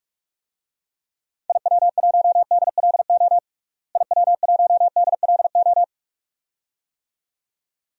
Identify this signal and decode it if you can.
Morse code, heard as a keyed tone, about 45 wpm, 700 Hz. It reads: IW1DLO IW1DLO
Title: DXCC-100 Call Signs - 6 Characters - Repeated 2x - 45wpm